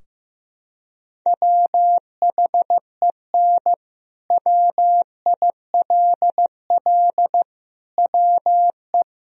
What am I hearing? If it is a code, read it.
Morse code, heard as a keyed tone, about 15 wpm, 700 Hz. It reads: WHEN WILL WE
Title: Sentences from Top 1000 Words - Review Part 3 - 15wpm